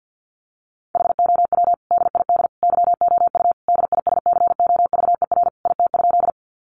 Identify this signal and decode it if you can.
Morse code, heard as a keyed tone, about 48 wpm, 700 Hz. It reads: HOW DID YOU DISCOVER IT?
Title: Sentences from Top 1100 Words 50wpm